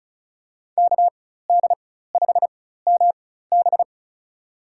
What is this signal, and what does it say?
Morse code, heard as a keyed tone, about 34 wpm, spaced as for 16 wpm, 700 Hz. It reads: KD5MB
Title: USA Call Signs - 4x Character Spacing - 35wpm